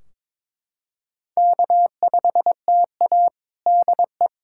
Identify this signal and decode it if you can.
Morse code, heard as a keyed tone, about 22 wpm, 700 Hz. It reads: K5TA DE
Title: K1USN SST QSO Practice 22wpm